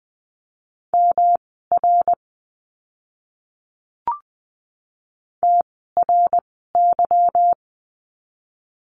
Morse code, heard as a keyed tone, about 20 wpm, spaced as for 15 wpm, 700 Hz. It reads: MR TRY
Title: 244 Lesson 39f - Letter z - Words 20wpm